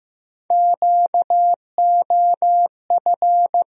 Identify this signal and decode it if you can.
Morse code, heard as a keyed tone, about 15 wpm, 700 Hz. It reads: QOF
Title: USA Call Sign Postfixes - 3 Letters 15wpm